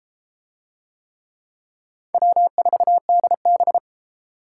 Morse code, heard as a keyed tone, about 30 wpm, 700 Hz. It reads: W4DB